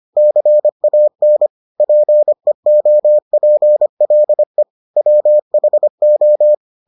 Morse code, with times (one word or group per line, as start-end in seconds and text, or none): 0.2-1.5 CAN
1.8-4.6 PEOPLE
5.0-6.5 WHO